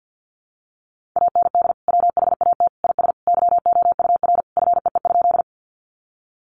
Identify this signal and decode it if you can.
Morse code, heard as a keyed tone, about 50 wpm, 700 Hz. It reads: AND WHAT IS YOUR FEE?
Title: Sentences from Top 1500 Words 50wpm